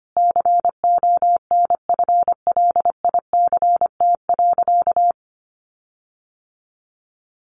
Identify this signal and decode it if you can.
Morse code, heard as a keyed tone, about 25 wpm, 700 Hz. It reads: CONFLICT.
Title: Sentences from Top 1300 Words 25wpm